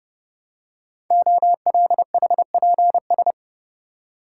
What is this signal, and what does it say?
Morse code, heard as a keyed tone, about 30 wpm, 700 Hz. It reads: OLHPS
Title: Code Groups - 5 Letters - 30wpm